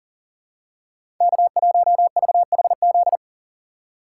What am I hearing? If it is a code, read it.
Morse code, heard as a keyed tone, about 40 wpm, 700 Hz. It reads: K1VHZ
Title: USA Call Signs - 5 Characters - 4x Character Spacing - 40wpm